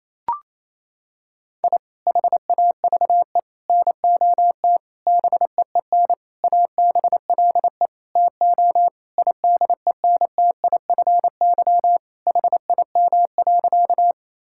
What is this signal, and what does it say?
Morse code, heard as a keyed tone, about 28 wpm, 700 Hz. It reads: I HAVE NOT BEEN ABLE TO IDENTIFY HIM.
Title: Sentences from Top 1000 Words - Review Part 6 - 28wpm